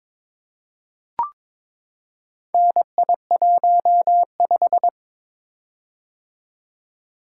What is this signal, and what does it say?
Morse code, heard as a keyed tone, about 22 wpm, 700 Hz. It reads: NI15
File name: practice / 22wpm / Four Letter-Number Combinations - 22wpm